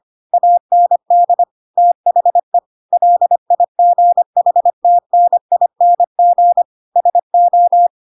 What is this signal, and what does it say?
Morse code, heard as a keyed tone, about 25 wpm, 700 Hz. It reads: AND THE LIGHTNING SO
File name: Pinocchio / Pinocchio ch7 25 25 ik1hge